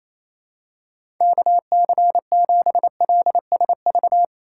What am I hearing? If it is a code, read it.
Morse code, heard as a keyed tone, about 27 wpm, 700 Hz. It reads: KC7LSV
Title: USA Call Signs - 6 Characters - 3x Character Spacing - 28wpm